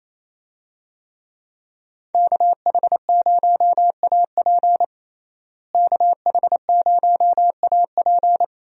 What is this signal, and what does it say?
Morse code, heard as a keyed tone, about 27 wpm, 700 Hz. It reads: KH0AP KH0AP
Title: USA Call Signs - 5 Characters - Repeated 3x - 28wpm